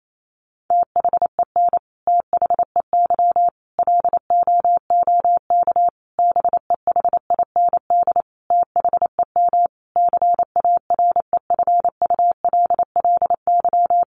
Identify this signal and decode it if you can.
Morse code, heard as a keyed tone, about 28 wpm, 700 Hz. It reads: THEN THEY LOOK BEHIND THEM CAREFULLY
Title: Sentences from Top 1300 Words 28wpm